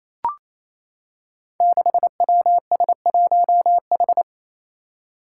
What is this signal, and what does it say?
Morse code, heard as a keyed tone, about 27 wpm, 700 Hz. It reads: 6WS1H
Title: Five Challenging Letter-Number Combinations 28wpm